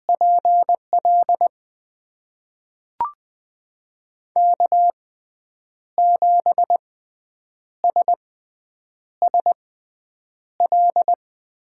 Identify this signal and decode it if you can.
Morse code, heard as a keyed tone, about 20 wpm, spaced as for 8 wpm, 700 Hz. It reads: PL K7SSL
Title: USA Call Signs - 5 Characters - 6x Character Spacing - 20wpm